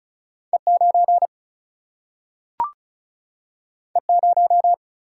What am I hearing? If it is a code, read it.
Morse code, heard as a keyed tone, about 36 wpm, 700 Hz. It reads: E9 E0
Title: Single-Letter Single-Number 35wpm